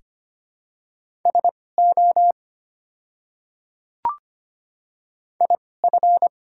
Morse code, heard as a keyed tone, about 25 wpm, spaced as for 18 wpm, 700 Hz. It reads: SO IF
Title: 2x Character Spacing - Top 100 Words - 25wpm